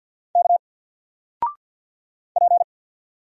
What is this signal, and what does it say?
Morse code, heard as a keyed tone, about 50 wpm, 700 Hz. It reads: K P
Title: Single Character (Letter:Number:Punctuation:Prosign) 50wpm